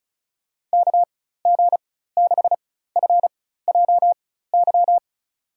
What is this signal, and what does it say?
Morse code, heard as a keyed tone, about 34 wpm, spaced as for 16 wpm, 700 Hz. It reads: KG6FJY
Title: USA Call Signs - 6 Characters - 4x Character Spacing - 35wpm